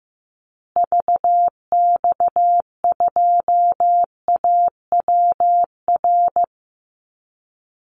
Morse code, heard as a keyed tone, about 15 wpm, 700 Hz. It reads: VX2AWR
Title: Call Signs Repeated 2x - 15wpm